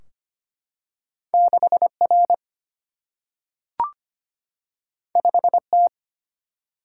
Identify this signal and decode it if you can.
Morse code, heard as a keyed tone, about 25 wpm, 700 Hz. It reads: 6 R 5 T
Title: Single-Letter Single-Number 25wpm